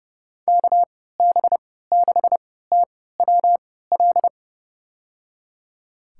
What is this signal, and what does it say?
Morse code, heard as a keyed tone, about 31 wpm, spaced as for 17 wpm, 700 Hz. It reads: KB6TWL
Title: USA Call Signs - 3x Character Spacing - 30wpm